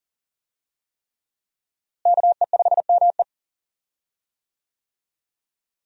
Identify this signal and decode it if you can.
Morse code, heard as a keyed tone, about 40 wpm, 700 Hz. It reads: KE5ME
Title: USA Call Signs - 5 Characters - Repeated 3x - 40wpm